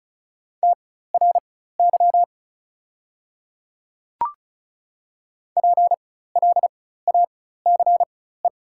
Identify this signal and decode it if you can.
Morse code, heard as a keyed tone, about 35 wpm, spaced as for 16 wpm, 700 Hz. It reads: TRY PLACE
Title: 4x Character Spacing - Top 200 Words - 35wpm